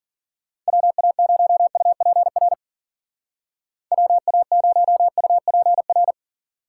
Morse code, heard as a keyed tone, about 53 wpm, 700 Hz. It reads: WA0UPR WA0UPR